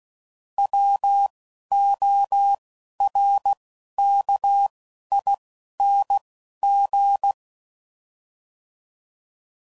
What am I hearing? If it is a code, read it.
Morse code, heard as a keyed tone, about 16 wpm, 800 Hz. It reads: W O R K I N G